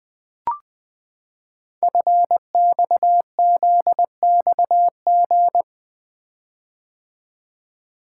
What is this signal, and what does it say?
Morse code, heard as a keyed tone, about 20 wpm, 700 Hz. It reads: FXZXG
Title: Code Groups - 5 Letters - 20wpm